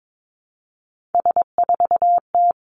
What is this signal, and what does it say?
Morse code, heard as a keyed tone, about 22 wpm, 700 Hz. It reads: S4T
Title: Three Challenging Letter-Number Combinations 22wpm